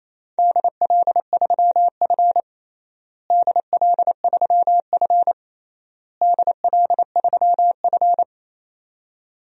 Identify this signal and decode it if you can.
Morse code, heard as a keyed tone, about 28 wpm, 700 Hz. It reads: DL3F DL3F DL3F
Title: DXCC-100 Call Signs - 4 Characters - Repeated 3x - 28wpm